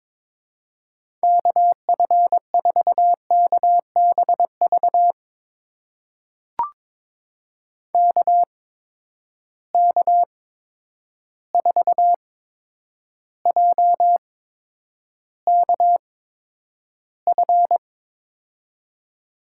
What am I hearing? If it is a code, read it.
Morse code, heard as a keyed tone, about 21 wpm, spaced as for 9 wpm, 700 Hz. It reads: KF4KBV KK4JKF
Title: USA Call Signs - 6 Characters - 8x Character Spacing - 22wpm